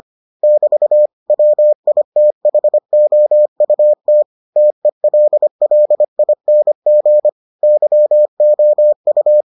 Morse code, heard as a keyed tone, about 25 wpm, 600 Hz. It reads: <BT> WITHOUT TELLING YOU